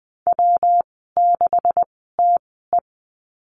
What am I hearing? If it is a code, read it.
Morse code, heard as a keyed tone, about 20 wpm, spaced as for 15 wpm, 700 Hz. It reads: W6TE
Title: 128 Lesson 21g - Number 6 - Call Signs 20wpm